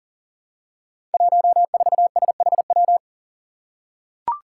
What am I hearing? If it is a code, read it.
Morse code, heard as a keyed tone, about 40 wpm, 700 Hz. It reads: 14SHW
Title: Five Challenging Letter-Number Combinations 40wpm